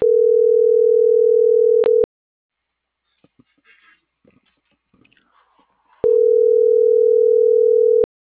Ringback tone (North America), 440 + 480 Hz.